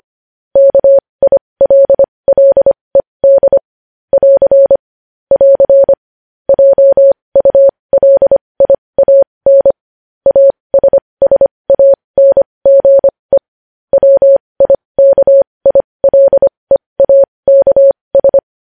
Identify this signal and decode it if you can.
Morse code, heard as a keyed tone, about 25 wpm, spaced as for 20 wpm, 550 Hz. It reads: KILLED <AR> <AR> JULIAN ASSANGE WIKILEAKS
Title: News Headlines at 20